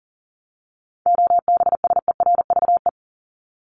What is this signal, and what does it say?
Morse code, heard as a keyed tone, about 40 wpm, 700 Hz. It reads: OBSERVE
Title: ED Suffix - Top 100 Words with Base Words 40wpm